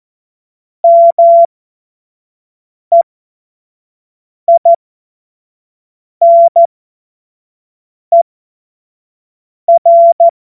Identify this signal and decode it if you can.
Morse code, heard as a keyed tone, about 14 wpm, spaced as for 5 wpm, 680 Hz. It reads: MEINER
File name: Morsekurs / Witze / Witz 15. November 2022 25 F70